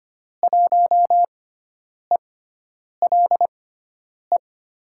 Morse code, heard as a keyed tone, about 25 wpm, spaced as for 9 wpm, 700 Hz. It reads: 1ELE
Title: USA Call Signs - 6 Characters - 6x Character Spacing - 25wpm